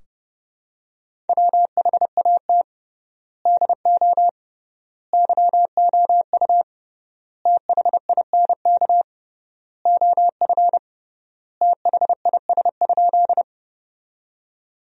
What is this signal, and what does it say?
Morse code, heard as a keyed tone, about 30 wpm, 700 Hz. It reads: WHAT DO YOU THINK OF THIS?
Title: 3x Word-Spacing - Sentences from Top 200 Words - 30wpm